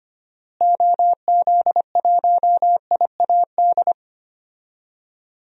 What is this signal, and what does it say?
Morse code, heard as a keyed tone, about 25 wpm, 700 Hz. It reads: OZ1IAD
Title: DXCC-100 Call Signs - 6 Characters - Repeated 2x - 25wpm